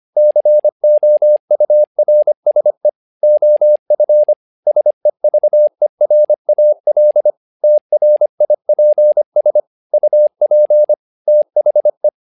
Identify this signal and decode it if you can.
Morse code, heard as a keyed tone, about 25 wpm, 600 Hz. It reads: COURSE OF SEVERAL TRIPS UP THE
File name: learning center / tower trouble / tower trouble 25wpm